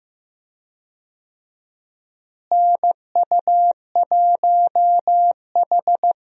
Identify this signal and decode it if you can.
Morse code, heard as a keyed tone, about 15 wpm, 700 Hz. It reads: NU1H
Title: USA Call Signs - 4 Characters - Repeated 3x - 15wpm